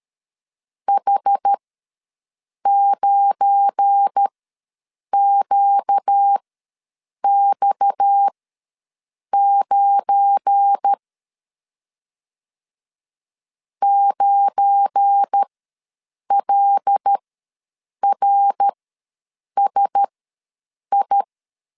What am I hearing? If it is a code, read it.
Morse code, heard as a keyed tone, about 13 wpm, spaced as for 7 wpm, 800 Hz. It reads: H9QX9 9LRSI
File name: LEZIONI / 7 GF9Z / 7 GF9 ws 12wpm 3344